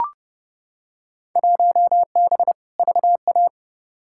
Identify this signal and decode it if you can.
Morse code, heard as a keyed tone, about 30 wpm, 700 Hz. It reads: 1B VA
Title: Field Day Exchanges Encore - 30wpm